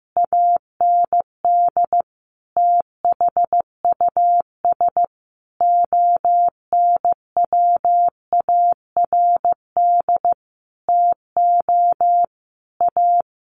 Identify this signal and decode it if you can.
Morse code, heard as a keyed tone, about 15 wpm, 700 Hz. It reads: AND THUS ONWARD TO A
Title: A Sound Of Thunder 15wpm